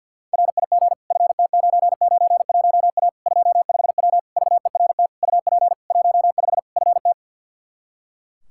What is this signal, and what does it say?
Morse code, heard as a keyed tone, about 45 wpm, 700 Hz. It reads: RIG FT991A 25W VERT UP 15 FT
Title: Practice QSO - Exchange 2 - Info - 50wpm